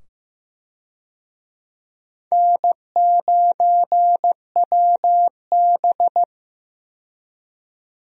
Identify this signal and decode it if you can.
Morse code, heard as a keyed tone, about 15 wpm, 700 Hz. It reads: N9WB
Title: USA Call Signs - 4 Characters - 15wpm